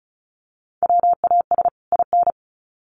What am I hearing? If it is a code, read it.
Morse code, heard as a keyed tone, about 34 wpm, 700 Hz. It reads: WAS IN
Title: Sets of 2 Words 35wpm